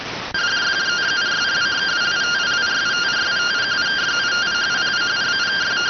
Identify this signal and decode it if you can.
Two-tone teleprinter signal, about 170 Hz shift.